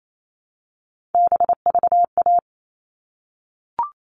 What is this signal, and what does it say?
Morse code, heard as a keyed tone, about 28 wpm, 700 Hz. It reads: BVA
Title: All Cut Numbers - 3-Digit 28wpm